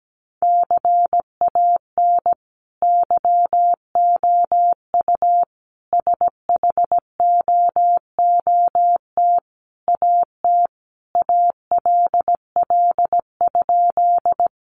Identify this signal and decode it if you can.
Morse code, heard as a keyed tone, about 17 wpm, 700 Hz. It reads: CAN YOU SHOOT AT ALL?
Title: Sentences from Top 1200 Words 17wpm